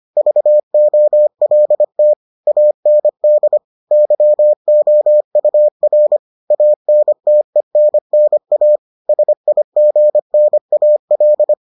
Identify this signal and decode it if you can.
Morse code, heard as a keyed tone, about 25 wpm, 600 Hz. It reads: VOLT AND YOUR ANTENNA SIGNAL